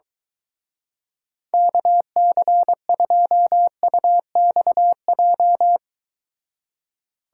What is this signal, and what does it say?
Morse code, heard as a keyed tone, about 23 wpm, 700 Hz. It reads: KC2UXJ